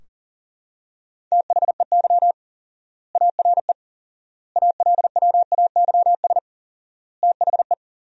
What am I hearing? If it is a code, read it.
Morse code, heard as a keyed tone, about 37 wpm, 700 Hz. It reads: THEY ARE ALWAYS THE
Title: 4x Word-Spacing - Sentences from Top 300 Words - 40wpm